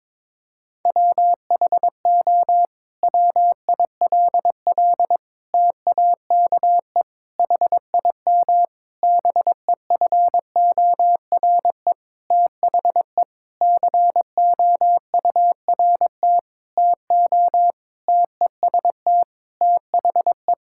Morse code, heard as a keyed tone, about 22 wpm, 700 Hz. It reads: WHO WILL TAKE HIM BEFORE THE COURT TO TEST THE
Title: Sentences from Top 1600 Words 22wpm